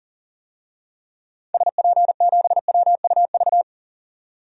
Morse code, heard as a keyed tone, about 39 wpm, 700 Hz. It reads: SP7WUV
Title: DXCC-100 Call Signs - 6 Characters - 40wpm